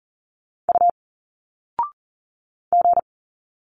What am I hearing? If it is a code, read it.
Morse code, heard as a keyed tone, about 40 wpm, 700 Hz. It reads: U G
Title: Single Letters 40wpm